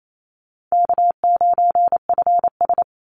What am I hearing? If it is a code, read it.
Morse code, heard as a keyed tone, about 28 wpm, 700 Hz. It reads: K9FS
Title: Easy Call Signs - 28wpm